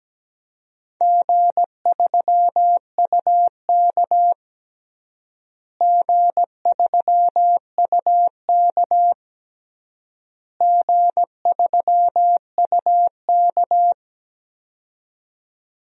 Morse code, heard as a keyed tone, about 17 wpm, 700 Hz. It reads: G3UK G3UK G3UK
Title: DXCC-100 Call Signs - 4 Characters - Repeated 3x - 17wpm